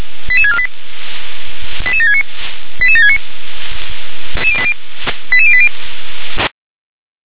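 Five-tone selective call (ZVEI), heard as quick runs of five tones, tones of about 80 ms. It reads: 88648 88769 88769 88080
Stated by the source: ZVEI five-tone call received on SDR.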